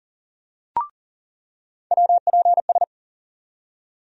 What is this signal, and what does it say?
Morse code, heard as a keyed tone, about 38 wpm, 700 Hz. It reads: WPS
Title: USA Call Sign Postfixes - 3 Letters 40wpm